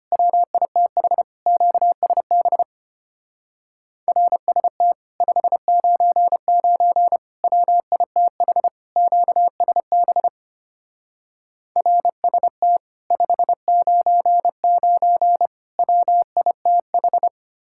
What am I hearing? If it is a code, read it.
Morse code, heard as a keyed tone, about 33 wpm, 700 Hz. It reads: WITH QSB RST 599 WITH QSB RST 599 WITH